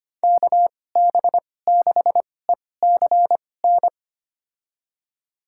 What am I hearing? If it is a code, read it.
Morse code, heard as a keyed tone, about 25 wpm, spaced as for 18 wpm, 700 Hz. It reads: KB6ECN
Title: USA Call Signs - 6 Characters - 2x Character Spacing - 25wpm